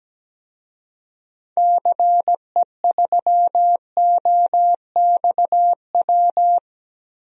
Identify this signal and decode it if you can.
Morse code, heard as a keyed tone, about 17 wpm, 700 Hz. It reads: CE3OXW
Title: DXCC-100 Call Signs - 6 Characters - Repeated 3x - 17wpm